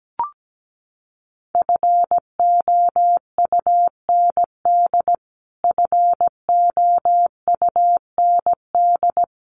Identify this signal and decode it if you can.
Morse code, heard as a keyed tone, about 17 wpm, 700 Hz. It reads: FOUND FOUND
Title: Top 300 Words Repeated 17wpm